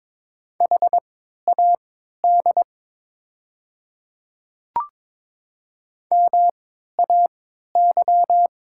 Morse code, heard as a keyed tone, about 22 wpm, spaced as for 13 wpm, 700 Hz. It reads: HAD MAY
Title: 3x Character Spacing - Top 100 Words - 22wpm